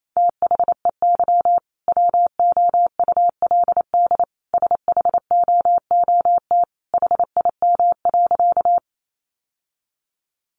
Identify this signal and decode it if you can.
Morse code, heard as a keyed tone, about 28 wpm, 700 Hz. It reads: THEY WOULD SHOOT HIM.
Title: Sentences from Top 1200 Words 28wpm